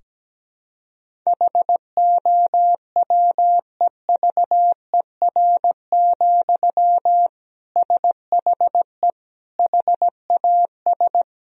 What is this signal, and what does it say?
Morse code, heard as a keyed tone, about 17 wpm, 700 Hz. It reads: HOWEVER, SHE HAS
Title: Sentences from Top 1000 Words - Review Part 2 - 17wpm